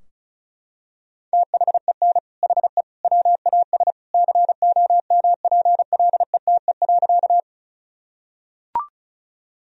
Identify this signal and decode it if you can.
Morse code, heard as a keyed tone, about 36 wpm, 700 Hz. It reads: THEN HE WAS COMPLETE.
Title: Sentences from Top 1000 Words - Review Part 2 - 35wpm